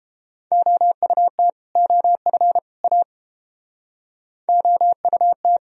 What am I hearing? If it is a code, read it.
Morse code, heard as a keyed tone, about 31 wpm, 700 Hz. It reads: OUT OF A OUT